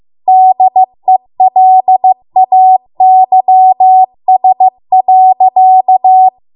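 Morse code, heard as a keyed tone, about 15 wpm, 750 Hz. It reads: DELAYS.
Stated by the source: Morse code practice at 15 wpm from the national amateur radio association's code archive.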